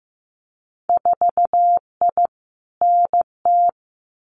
Morse code, heard as a keyed tone, about 15 wpm, 700 Hz. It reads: 4I NT
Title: Winter Field Day Exchanges Encore 15wpm